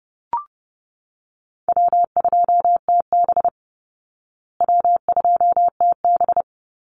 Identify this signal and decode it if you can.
Morse code, heard as a keyed tone, about 30 wpm, 700 Hz. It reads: W2TB W2TB
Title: Call Signs Repeated 2x - 30wpm